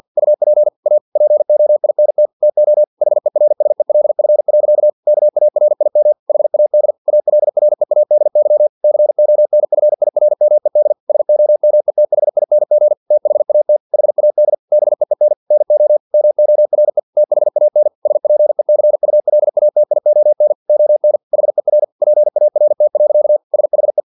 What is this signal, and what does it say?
Morse code, heard as a keyed tone, about 50 wpm, 600 Hz. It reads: UP A POINT TO HERSELF. KARIM HAD ALREADY CONFIRMED SOMETHING THAT HAD BEEN NO MORE THAN SPECULATION ON HER PART. SHE
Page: Never 49wpm